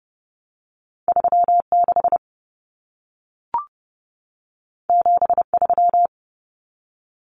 Morse code, heard as a keyed tone, about 30 wpm, 700 Hz. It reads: 36 73
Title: Numbers - 2-Digit 30wpm